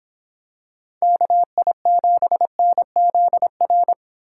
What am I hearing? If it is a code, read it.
Morse code, heard as a keyed tone, about 26 wpm, 700 Hz. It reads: KI7NZR